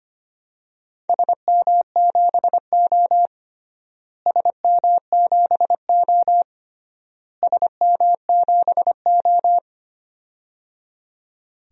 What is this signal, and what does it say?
Morse code, heard as a keyed tone, about 25 wpm, 700 Hz. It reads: SM7O SM7O SM7O
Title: DXCC-100 Call Signs - 4 Characters - Repeated 3x - 25wpm